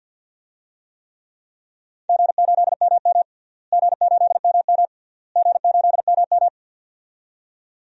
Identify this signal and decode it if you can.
Morse code, heard as a keyed tone, about 51 wpm, 700 Hz. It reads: G8MM G8MM G8MM
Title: DXCC-100 Call Signs - 4 Characters - Repeated 3x - 50wpm